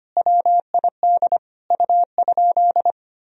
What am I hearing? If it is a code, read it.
Morse code, heard as a keyed tone, about 25 wpm, 700 Hz. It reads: WID U?
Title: Sentences from Top 100 Words and Abbreviations 25wpm